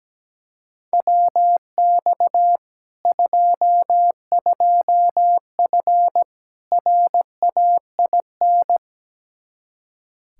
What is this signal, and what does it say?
Morse code, heard as a keyed tone, about 17 wpm, 700 Hz. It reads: WX 22F RAIN
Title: Practice QSO - Exchange 2 - Info - 17wpm